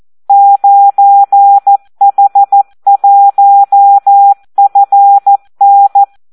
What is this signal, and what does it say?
Morse code, heard as a keyed tone, about 14 wpm, 800 Hz. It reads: 9H1FN
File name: Audio practica CW / SD 1 14